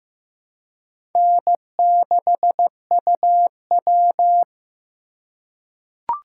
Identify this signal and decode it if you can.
Morse code, heard as a keyed tone, about 15 wpm, 700 Hz. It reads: N6UW
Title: USA Call Signs - 3x Character Spacing - 15wpm